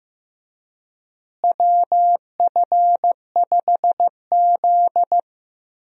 Morse code, heard as a keyed tone, about 15 wpm, 700 Hz. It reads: WF5Z
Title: USA Call Signs - 4 Characters - Repeated 3x - 15wpm